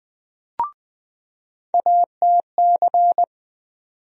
Morse code, heard as a keyed tone, about 20 wpm, 700 Hz. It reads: ATC
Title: USA Call Sign Postfixes - 3 Letters 20wpm